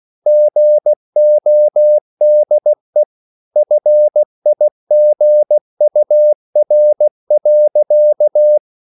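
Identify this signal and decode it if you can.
Morse code, heard as a keyed tone, about 16 wpm, 600 Hz. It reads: GODE FIGUR.
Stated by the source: Slow (QRS) Morse code at 16 wpm, from CW med Gnister.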